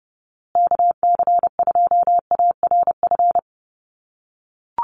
Morse code, heard as a keyed tone, about 30 wpm, 700 Hz. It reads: KC2ARF
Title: USA Call Signs Repeated 2x 30wpm